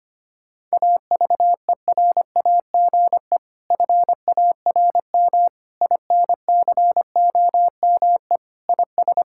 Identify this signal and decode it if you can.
Morse code, heard as a keyed tone, about 25 wpm, 700 Hz. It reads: AVERAGE FARM INCOME IS
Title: Sentences from Top 1000 Words - Review Part 6 - 25wpm